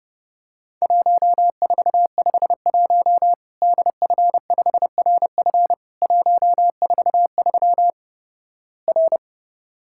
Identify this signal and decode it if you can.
Morse code, heard as a keyed tone, about 32 wpm, 700 Hz. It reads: 1451 DF5RF 143 R
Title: Worked All Europe DX Contest - QTC Exchange - 30wpm